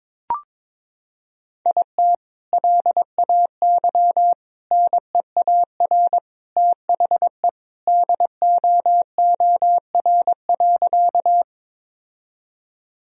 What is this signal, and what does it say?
Morse code, heard as a keyed tone, about 22 wpm, 700 Hz. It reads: IT LAY NEAR THE DOOR.
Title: Sentences from Top 1000 Words - Review Part 2 - 22wpm